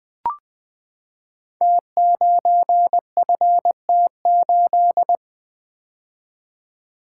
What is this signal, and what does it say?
Morse code, heard as a keyed tone, about 20 wpm, 700 Hz. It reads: T9FT8
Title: Code Groups - 5 Letter:Numbers - 20wpm